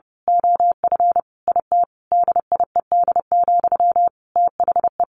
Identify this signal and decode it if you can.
Morse code, heard as a keyed tone, about 30 wpm, 700 Hz. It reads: OF IT DIED, THE
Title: A Sound Of Thunder 30wpm